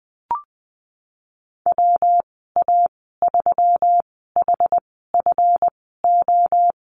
Morse code, heard as a keyed tone, about 20 wpm, spaced as for 15 wpm, 700 Hz. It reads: WA3HFO
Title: 173 Lesson 28g - Number 9 - Call Signs 20wpm